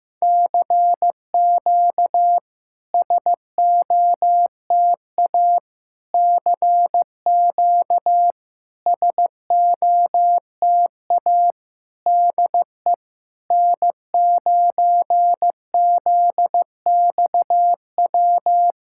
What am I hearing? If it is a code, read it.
Morse code, heard as a keyed tone, about 15 wpm, 700 Hz. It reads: CQ SOTA CQ SOTA DE N9ZXW
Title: SOTA QSO Practice 15wpm